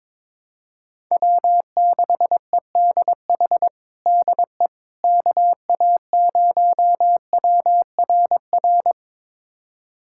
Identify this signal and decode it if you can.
Morse code, heard as a keyed tone, about 22 wpm, 700 Hz. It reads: W6EDH DE KA0WRR
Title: USA Callsign DE Callsign 22wpm